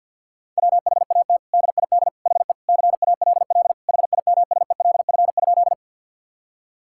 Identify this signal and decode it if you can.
Morse code, heard as a keyed tone, about 50 wpm, 700 Hz. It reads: WHAT DID HE CALL HIMSELF?
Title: Sentences from Top 1600 Words 50wpm